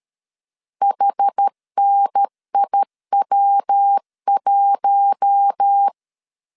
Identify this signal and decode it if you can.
Morse code, heard as a keyed tone, about 13 wpm, 800 Hz. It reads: HNIW1
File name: CW / DET / 1 6 1500C S B 12wpm 3431